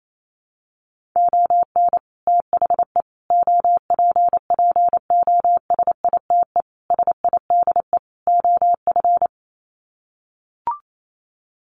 Morse code, heard as a keyed tone, about 28 wpm, 700 Hz. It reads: ON THE OPPOSITE SIDE OF
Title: Sets of 5 Words Encore2 28wpm